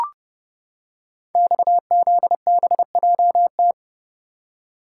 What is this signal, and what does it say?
Morse code, heard as a keyed tone, about 30 wpm, 700 Hz. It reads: XZBJT
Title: Code Groups - 5 Letter:Numbers - 30wpm